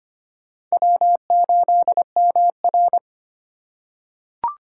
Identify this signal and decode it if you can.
Morse code, heard as a keyed tone, about 25 wpm, 700 Hz. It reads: W8MR
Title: USA Call Signs - 3x Character Spacing - 25wpm